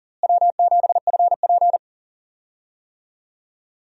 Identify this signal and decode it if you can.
Morse code, heard as a keyed tone, about 37 wpm, 700 Hz. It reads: W7FP
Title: USA Call Signs - 4 Characters - Repeated 3x - 40wpm